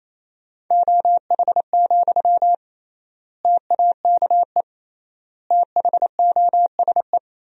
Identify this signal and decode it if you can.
Morse code, heard as a keyed tone, about 28 wpm, 700 Hz. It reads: OH, TAKE THOSE
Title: 3x Word-Spacing - Sentences from Top 500 Words - 28wpm